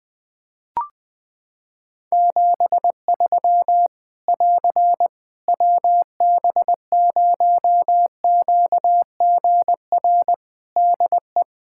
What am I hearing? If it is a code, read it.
Morse code, heard as a keyed tone, about 20 wpm, 700 Hz. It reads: 73 <AR> WB0QGR DE
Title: Practice QSO - Exchange 4 - Ending - 20wpm